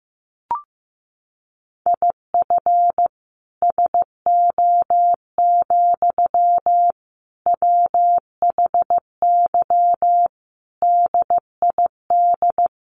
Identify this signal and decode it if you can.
Morse code, heard as a keyed tone, about 15 wpm, 700 Hz. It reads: IF SO, WHY DID
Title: Sentences from Top 1400 Words 15wpm